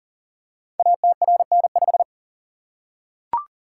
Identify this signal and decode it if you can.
Morse code, heard as a keyed tone, about 39 wpm, 700 Hz. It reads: ATRN5
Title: Code Groups - 5 Letter:Numbers - 40wpm